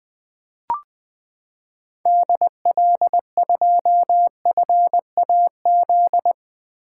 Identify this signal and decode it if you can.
Morse code, heard as a keyed tone, about 20 wpm, 700 Hz. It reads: DL2FAZ
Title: 2023 SOTA Activator Call Signs - 20wpm